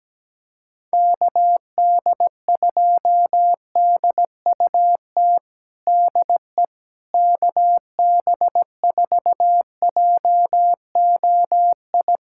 Morse code, heard as a keyed tone, about 17 wpm, 700 Hz. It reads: KD2DUT DE KB4JOI
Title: USA Callsign DE Callsign - 6 Characters - 17wpm